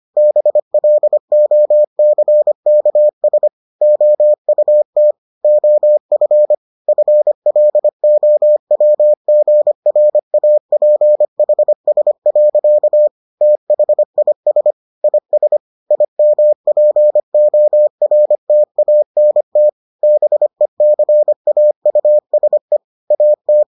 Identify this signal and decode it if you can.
Morse code, heard as a keyed tone, about 25 wpm, 600 Hz. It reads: BLOCKS OUT OF FLOWGRAPHS. THIS IS IMPORTANT BECAUSE AT